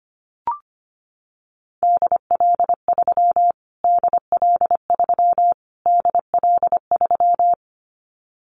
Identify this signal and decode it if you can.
Morse code, heard as a keyed tone, about 25 wpm, 700 Hz. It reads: DL3 DL3 DL3
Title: Callsign Prefixes for Top 25 DXCC - Repeated 3x - 25wpm